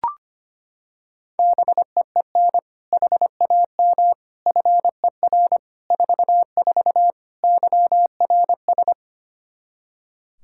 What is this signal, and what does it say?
Morse code, heard as a keyed tone, about 25 wpm, 700 Hz. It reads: BEEN HAM FER 44 YRS
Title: Practice QSO - Exchange 3 - More Info - 25wpm